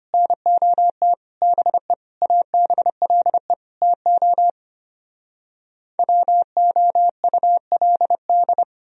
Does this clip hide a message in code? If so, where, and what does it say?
Yes: Morse code from 0.1 to 8.6 s, heard as a keyed tone, about 33 wpm, 700 Hz. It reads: NOT BE ABLE TO WOULD